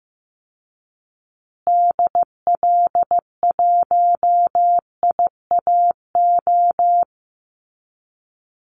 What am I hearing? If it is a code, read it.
Morse code, heard as a keyed tone, about 15 wpm, 700 Hz. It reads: DL1IAO
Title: Call Signs Repeated 2x - 15wpm